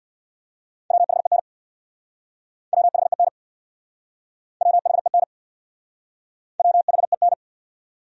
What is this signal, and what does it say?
Morse code, heard as a keyed tone, about 66 wpm, 700 Hz. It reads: WHEN WHEN WHEN WHEN